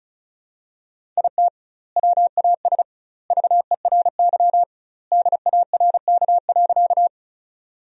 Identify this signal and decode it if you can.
Morse code, heard as a keyed tone, about 34 wpm, 700 Hz. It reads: IT WAS VERY DARK.
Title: 2x Word-Spacing - Sentences from Top 500 Words - 35wpm